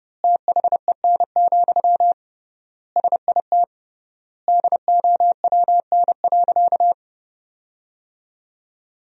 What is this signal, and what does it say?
Morse code, heard as a keyed tone, about 32 wpm, 700 Hz. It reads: THEN, SIT DOWN.
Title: 3x Word-Spacing - Sentences from Top 500 Words - 30wpm